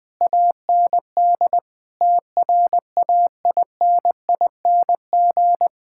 Morse code, heard as a keyed tone, about 20 wpm, 700 Hz. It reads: AND TRAINING
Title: Sentences from Top 1000 Words - Review Part 5 - 20wpm